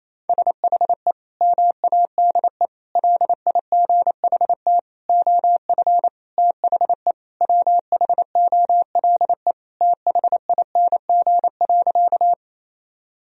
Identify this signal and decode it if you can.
Morse code, heard as a keyed tone, about 28 wpm, 700 Hz. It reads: SHE MADE LIGHT OF THE WHOLE THING.
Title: Sentences from Top 1000 Words - Review Part 1 - 28wpm